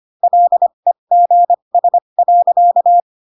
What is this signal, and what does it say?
Morse code, heard as a keyed tone, about 25 wpm, spaced as for 22 wpm, 700 Hz. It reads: LEGS.
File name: Pinocchio / Pinocchio ch2 18 25 ik1hge